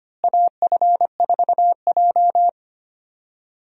Morse code, heard as a keyed tone, about 25 wpm, 700 Hz. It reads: AF4J
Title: USA Call Signs - 4 Characters - Repeated 2x - 25wpm